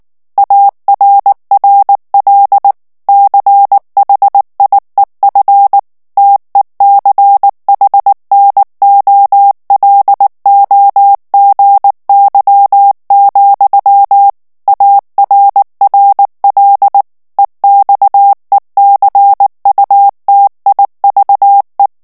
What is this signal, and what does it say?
Morse code, heard as a keyed tone, about 19 wpm, 800 Hz. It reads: ARRL CHIEF TECHNOLOGY, ARRL EXECUTIVE
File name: morse mp3 / ANGLAIS texte clair 3 vitesse 18 mots minute